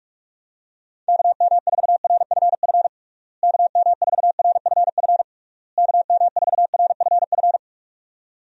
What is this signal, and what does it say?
Morse code, heard as a keyed tone, about 46 wpm, 700 Hz. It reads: KM4RFF KM4RFF KM4RFF
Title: USA Call Signs - 6 Characters - Repeated 3x - 45wpm